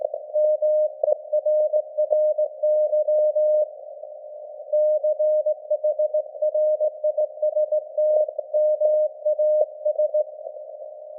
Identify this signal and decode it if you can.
Morse code, heard as a keyed tone, about 18 wpm, 610 Hz. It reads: MERRY CHRISTMAS
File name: laengstwellenempfang / SDRSharp 17kHz AF